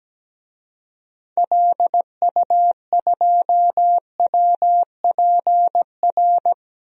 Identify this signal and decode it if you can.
Morse code, heard as a keyed tone, about 17 wpm, 700 Hz. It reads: LU2WPR
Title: DXCC-100 Call Signs - 6 Characters - Repeated 2x - 17wpm